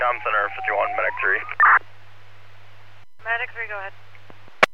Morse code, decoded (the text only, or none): none